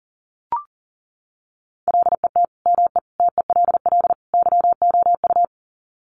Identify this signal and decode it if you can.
Morse code, heard as a keyed tone, about 40 wpm, 700 Hz. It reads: LET ME TELL YOU
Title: Sets of 4 Words 40wpm